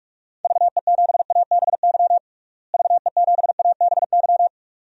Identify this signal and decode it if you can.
Morse code, heard as a keyed tone, about 46 wpm, 700 Hz. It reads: VE7ABY VE7ABY
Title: DXCC-100 Call Signs - 6 Characters - Repeated 2x - 45wpm